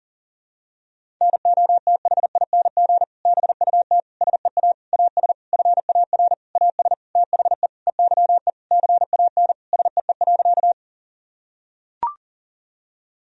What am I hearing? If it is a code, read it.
Morse code, heard as a keyed tone, about 40 wpm, 700 Hz. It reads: NOTHING BUT SEA AS FAR AS THE EYE CAN SEE.
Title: Sentences from Top 1000 Words - Review Part 1 - 40wpm